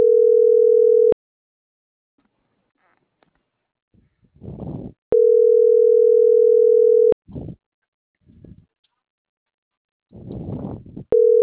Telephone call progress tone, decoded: ringback tone

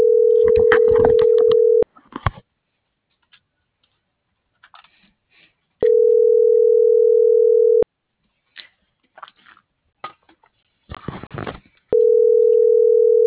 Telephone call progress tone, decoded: ringback tone

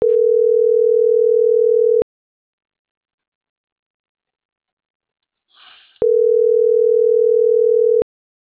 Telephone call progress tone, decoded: ringback tone

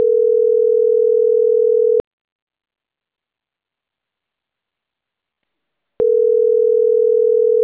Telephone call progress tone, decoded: ringback tone

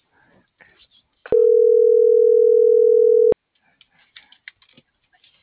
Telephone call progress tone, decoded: ringback tone

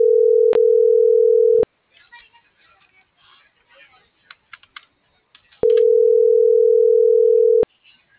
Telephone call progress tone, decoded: ringback tone

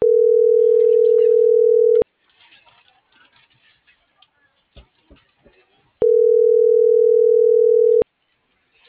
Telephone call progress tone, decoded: ringback tone

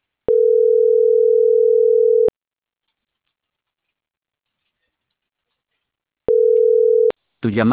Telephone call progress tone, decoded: ringback tone

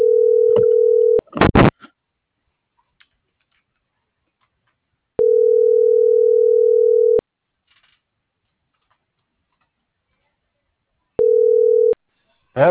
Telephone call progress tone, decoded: ringback tone

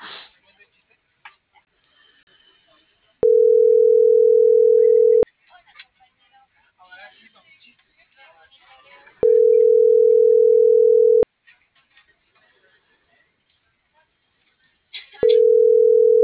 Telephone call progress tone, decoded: ringback tone

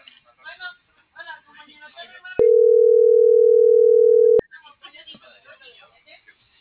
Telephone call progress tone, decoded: ringback tone

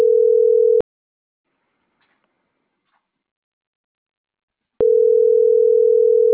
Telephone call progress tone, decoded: ringback tone

